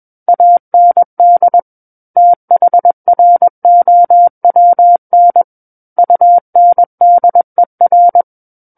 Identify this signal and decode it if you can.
Morse code, heard as a keyed tone, about 21 wpm, 700 Hz. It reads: AND THROWN UNDER
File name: Book One / 12-Chapter 12 - 21 WPM